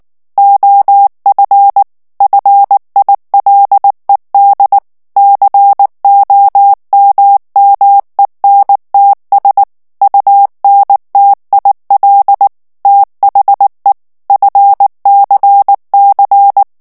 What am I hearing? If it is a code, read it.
Morse code, heard as a keyed tone, about 19 wpm, 800 Hz. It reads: OF FILED COMMENTS UNTIL THE FCC
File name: ANGLAIS texte clair 3 vitesse 18 mots minute